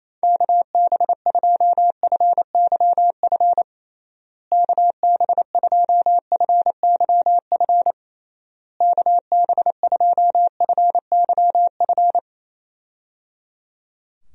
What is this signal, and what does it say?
Morse code, heard as a keyed tone, about 28 wpm, 700 Hz. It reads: KB2FYF KB2FYF KB2FYF
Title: USA Call Signs - 6 Characters - Repeated 3x - 28wpm